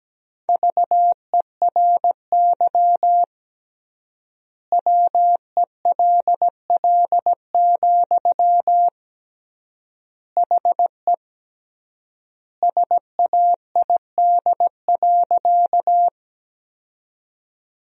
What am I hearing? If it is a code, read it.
Morse code, heard as a keyed tone, about 17 wpm, 700 Hz. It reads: VERY WELL, HE SAID.
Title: 3x Word-Spacing - Sentences from Top 200 Words - 17wpm